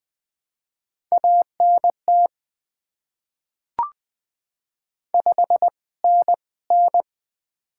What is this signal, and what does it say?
Morse code, heard as a keyed tone, about 21 wpm, spaced as for 17 wpm, 690 Hz. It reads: ANT 5NN
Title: Top 20 QSO Elements - 2x Character Spacing - 20wpm